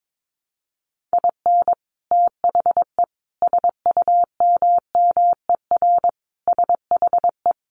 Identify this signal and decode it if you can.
Morse code, heard as a keyed tone, about 22 wpm, 700 Hz. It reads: IN THE SUMMER SHE
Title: Sentences from Top 1100 Words 22wpm